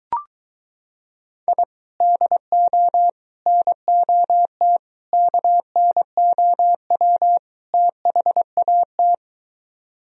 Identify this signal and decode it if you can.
Morse code, heard as a keyed tone, about 23 wpm, 700 Hz. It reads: I DO NOT KNOW THAT